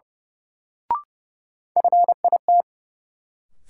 Morse code, heard as a keyed tone, about 30 wpm, 700 Hz. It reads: FIT
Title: Sentences from Top 800 Words - New Word Drill - 30wpm